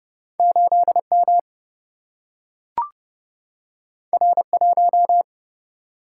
Morse code, heard as a keyed tone, about 30 wpm, 700 Hz. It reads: 8 M R 1
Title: Single-Letter Single-Number 30wpm